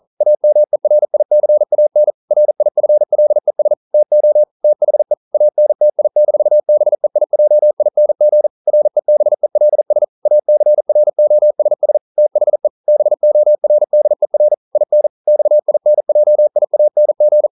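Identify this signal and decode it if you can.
Morse code, heard as a keyed tone, about 42 wpm, 600 Hz. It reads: AMERICAN RIFLES TO THE ANTI-BEIJING REBELS ACROSS THE BORDER IN XINJIANG
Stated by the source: Morse code practice at 41 wpm, from CW med Gnister.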